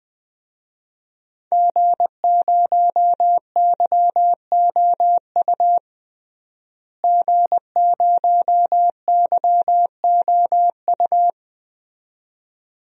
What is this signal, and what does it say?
Morse code, heard as a keyed tone, about 20 wpm, 700 Hz. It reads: G0YOU G0YOU
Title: DXCC-100 Call Signs - 5 Characters - Repeated 2x - 20wpm